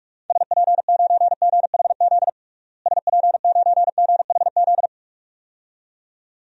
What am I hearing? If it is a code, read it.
Morse code, heard as a keyed tone, about 47 wpm, 700 Hz. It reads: SP9GHZ SP9GHZ
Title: DXCC-100 Call Signs - 6 Characters - Repeated 2x - 45wpm